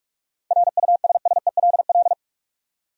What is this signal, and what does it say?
Morse code, heard as a keyed tone, about 44 wpm, 700 Hz. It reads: RUSSELL
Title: USA Common Names - 45wpm